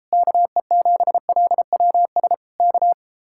Morse code, heard as a keyed tone, about 33 wpm, 700 Hz. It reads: KE7LWS K